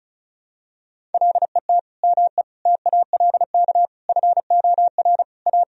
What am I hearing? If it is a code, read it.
Morse code, heard as a keyed tone, about 36 wpm, 700 Hz. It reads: LET ME TALK FOR A
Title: Sentences from Top 1000 Words - Review Part 1 - 35wpm